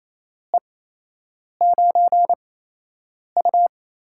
Morse code, heard as a keyed tone, about 28 wpm, spaced as for 8 wpm, 700 Hz. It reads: E9U
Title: USA Call Signs - 4 Characters - 8x Character Spacing - 28wpm